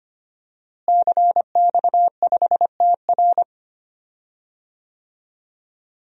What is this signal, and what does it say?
Morse code, heard as a keyed tone, about 25 wpm, 700 Hz. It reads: CX5TR
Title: DXCC-100 Call Signs - 5 Characters - Repeated 3x - 25wpm